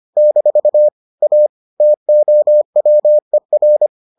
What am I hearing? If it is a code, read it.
Morse code, heard as a keyed tone, about 25 wpm, 600 Hz. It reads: - A TOWER